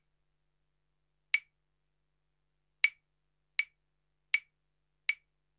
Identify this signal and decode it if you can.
There is no code.